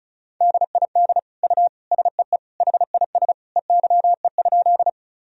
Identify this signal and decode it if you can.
Morse code, heard as a keyed tone, about 34 wpm, 700 Hz. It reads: DID U SEE HIS EYE?
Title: Sentences from Top 300 Words and Abbreviations 35wpm